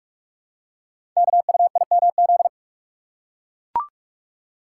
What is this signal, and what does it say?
Morse code, heard as a keyed tone, about 44 wpm, 700 Hz. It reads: KUIMZ
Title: Code Groups - 5 Letters - 45wpm